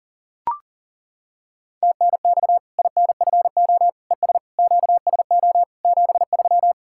Morse code, heard as a keyed tone, about 39 wpm, 700 Hz. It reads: TNX INFO ES QSO 73
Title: Practice QSO - Exchange 4 - Ending - 40wpm